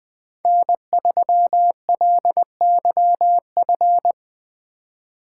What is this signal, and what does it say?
Morse code, heard as a keyed tone, about 24 wpm, 700 Hz. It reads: N3LYF
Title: Speed Racing - USA 1x3 Call Signs 20wpm